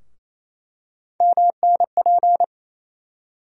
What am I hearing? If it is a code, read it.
Morse code, heard as a keyed tone, about 28 wpm, 700 Hz. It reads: MNP
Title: USA Call Sign Postfixes - 3 Letters 28wpm